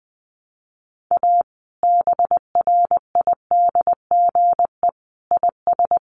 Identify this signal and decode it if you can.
Morse code, heard as a keyed tone, about 20 wpm, 700 Hz. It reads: A BRIDGE IS
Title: Sentences from Top 1200 Words 20wpm